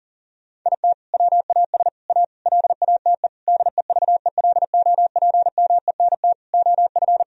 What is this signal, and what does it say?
Morse code, heard as a keyed tone, about 40 wpm, 700 Hz. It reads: IT WAS A LATE DEVELOPMENT OF
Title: Sentences from Top 1000 Words - Review Part 4 - 40wpm